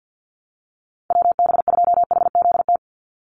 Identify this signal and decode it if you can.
Morse code, heard as a keyed tone, about 50 wpm, 700 Hz. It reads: WB2HZT
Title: USA Call Signs 50wpm